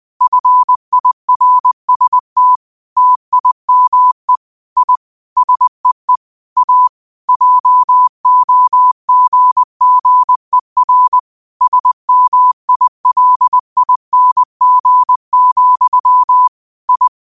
Morse code, heard as a keyed tone, about 20 wpm, 1000 Hz. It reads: FIRST TIME I SEE A JOGGER SMILING, I
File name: qotdfiles / 20wpm